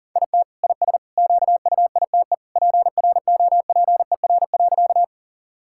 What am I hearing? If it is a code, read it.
Morse code, heard as a keyed tone, about 40 wpm, 690 Hz. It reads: IT IS QUITE PROPER.
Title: Sentences from Top 1000 Words - Review Part 5 - 40wpm